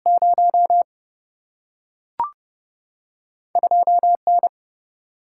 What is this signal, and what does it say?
Morse code, heard as a keyed tone, about 30 wpm, 700 Hz. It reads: 0 2 N
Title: Single-Letter Single-Number 30wpm